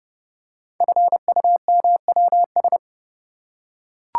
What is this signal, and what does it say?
Morse code, heard as a keyed tone, about 30 wpm, 700 Hz. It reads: FUMWS
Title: Code Groups - 5 Letters - 30wpm